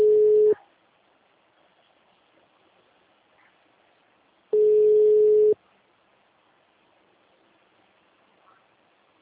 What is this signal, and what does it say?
Ringback tone (Europe, 425 Hz).
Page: Jundiai - SP